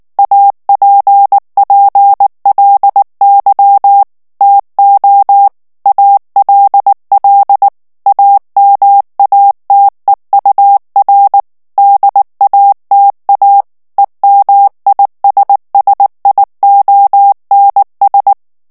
Morse code, heard as a keyed tone, about 19 wpm, 800 Hz. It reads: APPLY TO ALL AMATEUR DATA EMISSIONS
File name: ANGLAIS texte clair 3 vitesse 18 mots minute